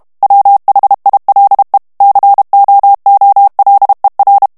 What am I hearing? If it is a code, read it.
Morse code, heard as a keyed tone, about 32 wpm, 800 Hz. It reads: WHILE COOLER
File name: ANGLAIS texte clair 2 vitesse 30 mots minute